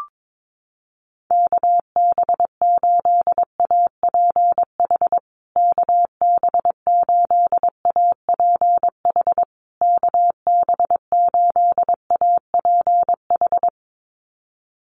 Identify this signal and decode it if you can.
Morse code, heard as a keyed tone, about 22 wpm, 700 Hz. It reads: KB8APH KB8APH KB8APH
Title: USA Call Signs Repeated 3x 22wpm